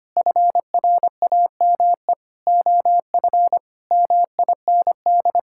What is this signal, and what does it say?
Morse code, heard as a keyed tone, about 25 wpm, 700 Hz. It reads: FRAME OF MIND
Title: Sentences from Top 1000 Words - Review Part 6 - 25wpm